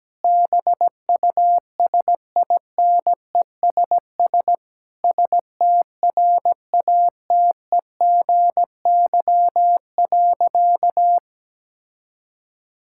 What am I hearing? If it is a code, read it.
Morse code, heard as a keyed tone, about 17 wpm, 700 Hz. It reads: BUSINESS STRATEGY.
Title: Sentences from Top 1000 Words - Review Part 6 - 17wpm